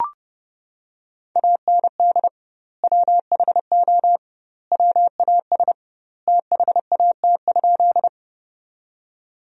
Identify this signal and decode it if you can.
Morse code, heard as a keyed tone, about 30 wpm, 700 Hz. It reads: AND WHO WAS THAT?
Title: 2x Word-Spacing - Sentences from Top 100 Words - 30wpm